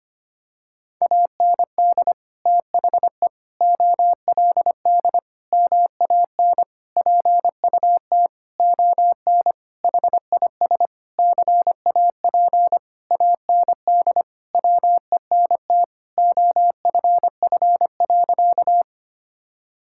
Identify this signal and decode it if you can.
Morse code, heard as a keyed tone, about 25 wpm, 700 Hz. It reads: AND THE OLD MAN PUT ON HIS CAP AND WENT OFF.
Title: Sentences from Top 1600 Words 25wpm